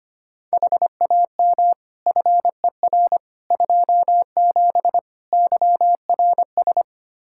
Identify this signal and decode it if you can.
Morse code, heard as a keyed tone, about 25 wpm, 700 Hz. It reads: HAM FER 27 YRS
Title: Practice QSO - Exchange 3 - More Info - 25wpm